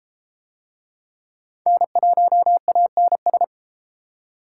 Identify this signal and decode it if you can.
Morse code, heard as a keyed tone, about 30 wpm, 700 Hz. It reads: N1ANS